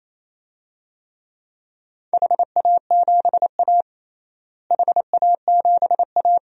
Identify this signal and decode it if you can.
Morse code, heard as a keyed tone, about 28 wpm, 700 Hz. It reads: HA7A HA7A
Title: DXCC-100 Call Signs - 4 Characters - Repeated 3x - 28wpm